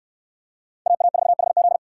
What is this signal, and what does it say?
Morse code, heard as a keyed tone, about 69 wpm, 700 Hz. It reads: AA4HZ